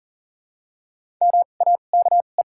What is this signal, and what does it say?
Morse code, heard as a keyed tone, about 40 wpm, spaced as for 29 wpm, 700 Hz. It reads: MAKE
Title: 2x Character Spacing - Top 100 Words - 40wpm